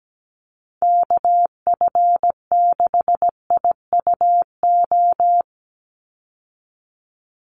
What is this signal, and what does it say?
Morse code, heard as a keyed tone, about 17 wpm, 700 Hz. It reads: KF6IUO
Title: USA Call Signs 17wpm